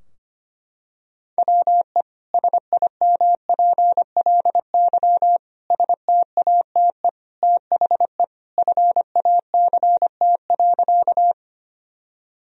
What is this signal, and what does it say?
Morse code, heard as a keyed tone, about 25 wpm, 700 Hz. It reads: WE SIMPLY STATE THE FACT.
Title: Sentences from Top 1000 Words - Review Part 4 - 25wpm